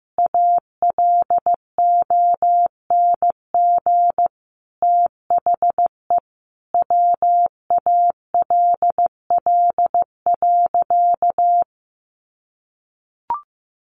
Morse code, heard as a keyed tone, about 15 wpm, 700 Hz. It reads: ALONG THE WALL.
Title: Sentences from Top 1100 Words 15wpm